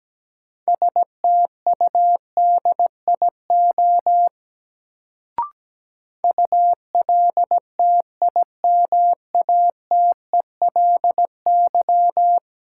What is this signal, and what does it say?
Morse code, heard as a keyed tone, about 17 wpm, 700 Hz. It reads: STUDIO ULTIMATELY
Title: Sentences from Top 1200 Words - New Word Drill Alt - 17wpm